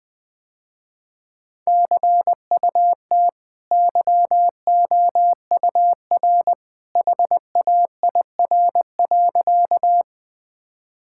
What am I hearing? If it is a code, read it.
Morse code, heard as a keyed tone, about 21 wpm, 700 Hz. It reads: CUT YOUR HAIR.
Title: Easy Sentences with Hints - 20wpm